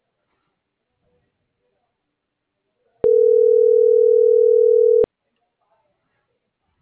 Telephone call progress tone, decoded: ringback tone

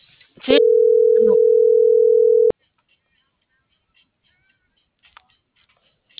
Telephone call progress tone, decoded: ringback tone

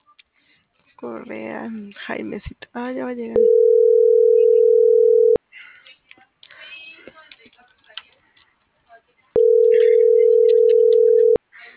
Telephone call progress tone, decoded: ringback tone

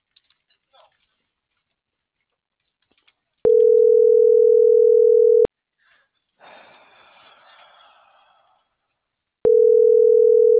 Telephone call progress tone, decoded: ringback tone

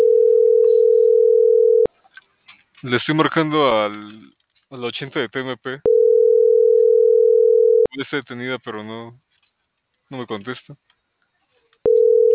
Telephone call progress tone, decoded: ringback tone

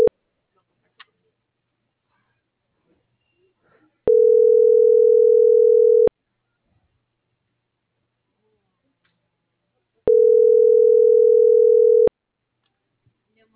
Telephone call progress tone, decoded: ringback tone

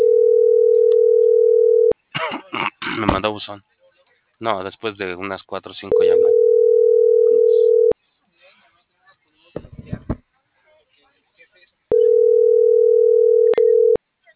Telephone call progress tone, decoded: ringback tone

